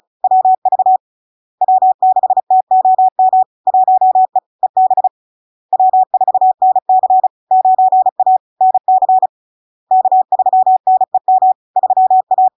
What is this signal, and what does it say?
Morse code, heard as a keyed tone, about 36 wpm, 750 Hz. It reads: WV W6TOM 1E EB W4NC 9A NC K3NEM 3A
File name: LARK-FD-2022-Log-in-CW-35wpm-750-Hz